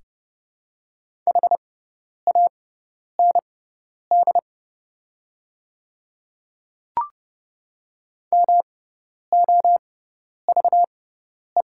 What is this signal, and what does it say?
Morse code, heard as a keyed tone, about 30 wpm, spaced as for 10 wpm, 700 Hz. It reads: HAND MOVE
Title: 6x Character Spacing - Top 200 Words - 30wpm